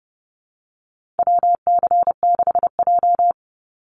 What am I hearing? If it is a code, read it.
Morse code, heard as a keyed tone, about 30 wpm, 700 Hz. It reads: WC6J
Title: NA SOTA Chaser Callsigns 30wpm